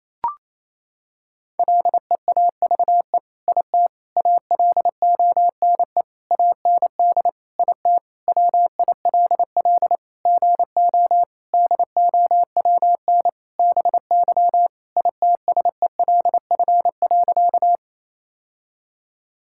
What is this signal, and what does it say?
Morse code, heard as a keyed tone, about 28 wpm, 700 Hz. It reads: LEAVE IT ALONE AND IT WILL GO DOWN BY ITSELF.
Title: Sentences from Top 1000 Words - Review Part 5 - 28wpm